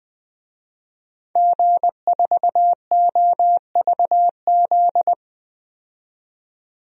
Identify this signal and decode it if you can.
Morse code, heard as a keyed tone, about 20 wpm, 700 Hz. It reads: G4OVZ
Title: DXCC-100 Call Signs - 5 Characters - 20wpm